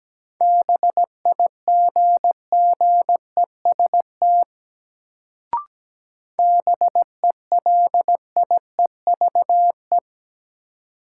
Morse code, heard as a keyed tone, about 17 wpm, 700 Hz. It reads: BIGGEST BELIEVE
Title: Sets of 4 Words - New Word Drill - 17wpm